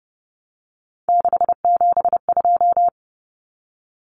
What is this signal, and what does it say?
Morse code, heard as a keyed tone, about 30 wpm, 700 Hz. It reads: 672
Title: Numbers - 3-Digit 30wpm